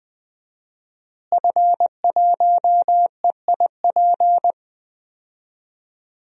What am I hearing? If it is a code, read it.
Morse code, heard as a keyed tone, about 20 wpm, 700 Hz. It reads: F1EIP
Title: DXCC-100 Call Signs - 5 Characters - Repeated 3x - 20wpm